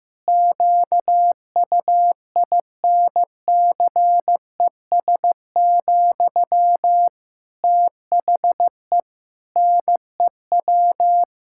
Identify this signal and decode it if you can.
Morse code, heard as a keyed tone, about 15 wpm, 700 Hz. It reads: QUINCES, THE NEW
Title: Pangram Challenge v2 - 15wpm